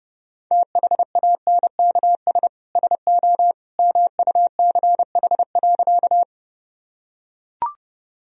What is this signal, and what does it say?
Morse code, heard as a keyed tone, about 30 wpm, 700 Hz. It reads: THANKS SO MUCH.
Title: Sentences from Top 1000 Words - Review Part 3 - 30wpm